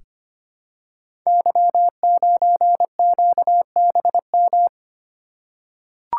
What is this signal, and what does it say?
Morse code, heard as a keyed tone, about 25 wpm, 700 Hz. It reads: Y9QBM
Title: Code Groups - 5 Letter:Numbers - 25wpm